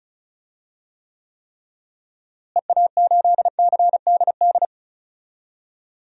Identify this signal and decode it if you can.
Morse code, heard as a keyed tone, about 36 wpm, 700 Hz. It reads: EA8CDD
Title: DXCC-100 Call Signs - 6 Characters - Repeated 3x - 35wpm